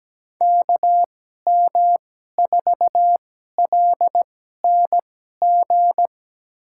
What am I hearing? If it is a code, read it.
Morse code, heard as a keyed tone, about 17 wpm, spaced as for 12 wpm, 700 Hz. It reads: KM4LNG
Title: USA Call Signs - 6 Characters - 2x Character Spacing - 17wpm